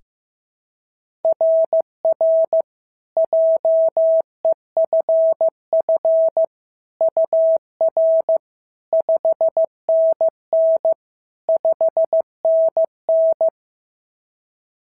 Morse code, heard as a keyed tone, about 14 wpm, 650 Hz. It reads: RR JEFF UR 5NN 5NN
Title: SOTA QSO Practice 15wpm